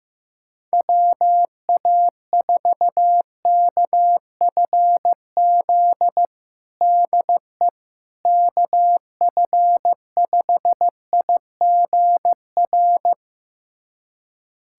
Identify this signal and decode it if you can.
Morse code, heard as a keyed tone, about 15 wpm, 700 Hz. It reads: WA4KFZ DE KF5IGR
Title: USA Callsign DE Callsign - 6 Characters - 15wpm